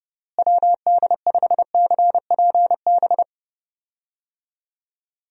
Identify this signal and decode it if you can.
Morse code, heard as a keyed tone, about 31 wpm, 700 Hz. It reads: WD5CPB
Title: USA Call Signs - 6 Characters - Repeated 2x - 30wpm